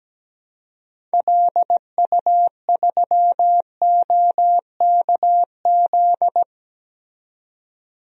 Morse code, heard as a keyed tone, about 17 wpm, 700 Hz. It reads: LU3OKZ
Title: DXCC-100 Call Signs - 6 Characters - 17wpm